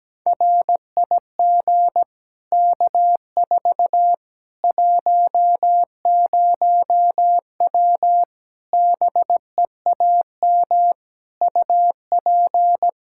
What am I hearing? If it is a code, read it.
Morse code, heard as a keyed tone, about 17 wpm, 700 Hz. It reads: RIG K4 10W BEAM UP
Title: K1USN SST QSO Practice 17wpm